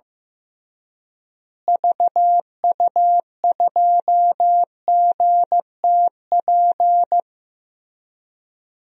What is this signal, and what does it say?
Morse code, heard as a keyed tone, about 16 wpm, 700 Hz. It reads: VU2GTP
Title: DXCC-100 Call Signs - 6 Characters - Repeated 3x - 15wpm